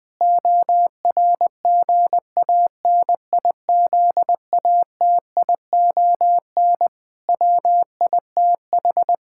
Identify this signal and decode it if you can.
Morse code, heard as a keyed tone, about 20 wpm, 700 Hz. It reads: ORGANIZATION WITH
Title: Sentences from Top 1000 Words - Review Part 5 - 20wpm